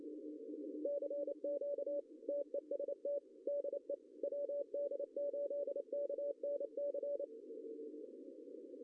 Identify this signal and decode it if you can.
Morse code, heard as a keyed tone, about 28 wpm, 570 Hz. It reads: CQ TEST DE WD8KNC
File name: CW / cqww-cw-k5tr